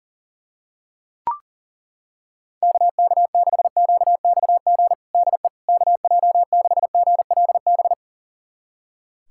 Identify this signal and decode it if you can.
Morse code, heard as a keyed tone, about 40 wpm, 700 Hz. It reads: KK6QXG DE KJ6GLB
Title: USA Callsign DE Callsign - 6 Characters - 40wpm